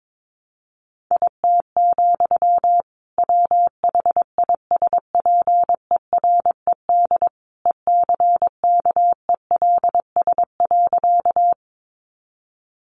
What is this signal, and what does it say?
Morse code, heard as a keyed tone, about 22 wpm, 700 Hz. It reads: IT, WHISPERED ECKELS.
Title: A Sound Of Thunder 22wpm